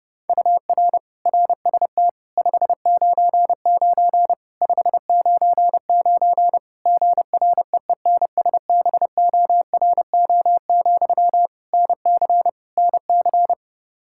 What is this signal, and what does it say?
Morse code, heard as a keyed tone, about 28 wpm, 700 Hz. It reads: UR RST 599 599 GREENSBORO, NC NC
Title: Practice QSO - Exchange 1 - Beginning - 30wpm